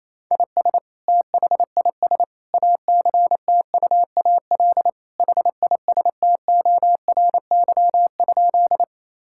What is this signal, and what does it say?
Morse code, heard as a keyed tone, about 28 wpm, 700 Hz. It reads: IS THIS ACTUAL HISTORY?
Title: Sentences from Top 1000 Words - Review Part 5 - 28wpm